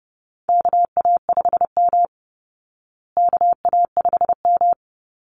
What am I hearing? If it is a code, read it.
Morse code, heard as a keyed tone, about 30 wpm, 700 Hz. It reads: KA5M KA5M
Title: Call Signs Repeated 2x - 30wpm